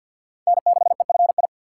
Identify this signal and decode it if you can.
Morse code, heard as a keyed tone, about 47 wpm, 700 Hz. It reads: N6EFI
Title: USA Call Signs - 5 Characters - 3x Character Spacing - 50wpm